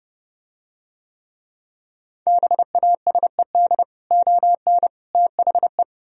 Morse code, heard as a keyed tone, about 29 wpm, 700 Hz. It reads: BASED ON THE